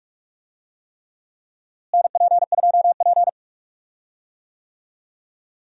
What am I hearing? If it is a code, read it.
Morse code, heard as a keyed tone, about 43 wpm, 700 Hz. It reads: NP2P
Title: USA Call Signs - 4 Characters - Repeated 3x - 45wpm